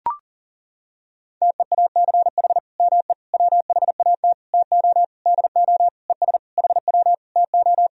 Morse code, heard as a keyed tone, about 37 wpm, 700 Hz. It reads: TEACH ME WHAT TO DO ES HW TO
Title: Sentences from Top 500 Words and Abbreviations 40wpm